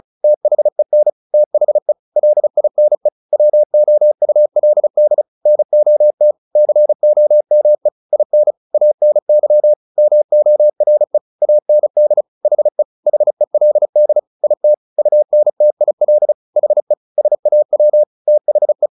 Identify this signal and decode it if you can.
Morse code, heard as a keyed tone, about 37 wpm, 600 Hz. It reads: THEN THE LINE WOULD NOT COME IN ANY MORE AND HE HELD IT UNTIL HE SAW THE
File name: the old man and the sea 2 35wpm